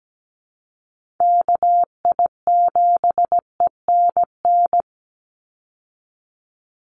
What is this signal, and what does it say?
Morse code, heard as a keyed tone, about 17 wpm, 700 Hz. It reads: KI7ENN
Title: USA Call Signs 17wpm